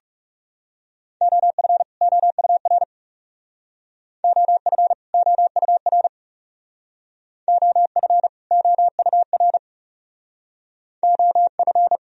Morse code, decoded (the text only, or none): OF OUR OF OUR OF OUR OF